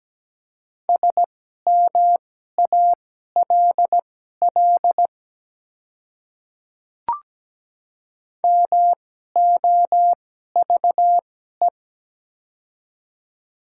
Morse code, heard as a keyed tone, about 17 wpm, spaced as for 12 wpm, 700 Hz. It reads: SMALL MOVE
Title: 2x Character Spacing - Top 300 Words - 17wpm